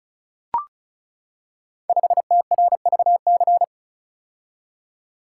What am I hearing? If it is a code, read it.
Morse code, heard as a keyed tone, about 36 wpm, 700 Hz. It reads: 5TRVC
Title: Code Groups - 5 Letter:Numbers - 35wpm